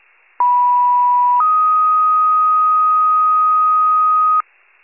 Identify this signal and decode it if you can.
Two-tone page, heard as a one-second tone, then a three-second tone, about 980 then 1250 Hz. It is heard from 0.4 to 4.4 s.